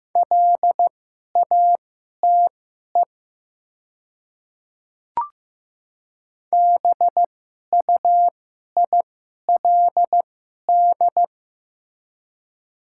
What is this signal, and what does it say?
Morse code, heard as a keyed tone, about 15 wpm, spaced as for 11 wpm, 700 Hz. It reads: LATE BUILD
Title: 2x Character Spacing - Top 500 Words - 15wpm